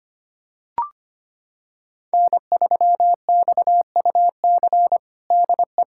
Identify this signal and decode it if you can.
Morse code, heard as a keyed tone, about 25 wpm, 700 Hz. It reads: N3XUC DE
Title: USA Callsign DE Callsign - 5 Characters - 25wpm